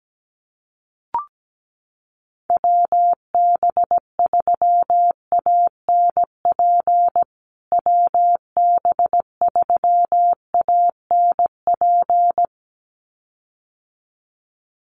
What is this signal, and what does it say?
Morse code, heard as a keyed tone, about 17 wpm, 700 Hz. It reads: WB3ANP WB3ANP
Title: USA Call Signs Repeated 2x 17wpm